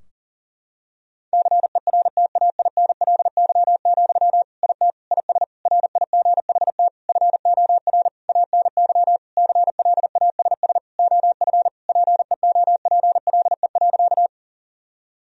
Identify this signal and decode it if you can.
Morse code, heard as a keyed tone, about 40 wpm, 700 Hz. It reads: CERTAINLY, IT IS RIGHT FOR ANY CLASS OF PEOPLE.
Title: Sentences from Top 1000 Words - Review Part 2 - 40wpm